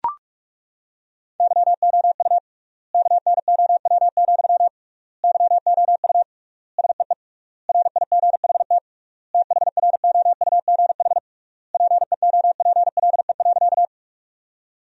Morse code, heard as a keyed tone, about 41 wpm, 700 Hz. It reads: YOU KNOW, YOU SEE RIGHT THROUGH PEOPLE.
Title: 3x Word-Spacing - Sentences from Top 200 Words - 45wpm